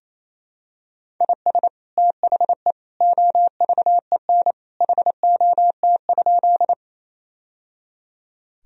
Morse code, heard as a keyed tone, about 28 wpm, 700 Hz. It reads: IS THE OVEN HOT?
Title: Sentences from Top 1000 Words - Review Part 4 - 28wpm